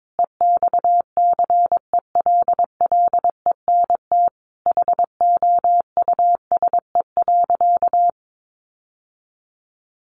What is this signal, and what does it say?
Morse code, heard as a keyed tone, about 22 wpm, 700 Hz. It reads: EXCELLENT HOUSE.
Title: Sentences from Top 1000 Words 22wpm